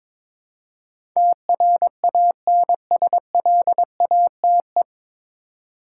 Morse code, heard as a keyed tone, about 21 wpm, 700 Hz. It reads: TRANSLATE
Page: Morse Code Ninja practice 22wpm